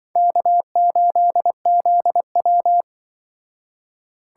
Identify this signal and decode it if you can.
Morse code, heard as a keyed tone, about 25 wpm, 700 Hz. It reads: K8ZW
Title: Speed Racing - USA 1x2 Call Signs 20wpm